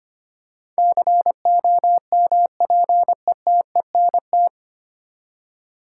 Morse code, heard as a keyed tone, about 25 wpm, 700 Hz. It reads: COMPETENT
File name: Nine Letter Words - 25wpm